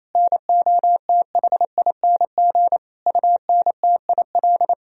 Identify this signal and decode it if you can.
Morse code, heard as a keyed tone, about 28 wpm, 700 Hz. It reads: NOTHING UNTIL
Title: Sentences from Top 500 Words and Abbreviations 28wpm